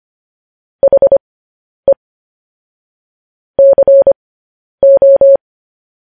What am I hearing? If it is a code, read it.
Morse code, heard as a keyed tone, about 25 wpm, spaced as for 10 wpm, 550 Hz. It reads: HE CO
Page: news-headlines-in-morse-code-at-10-wpm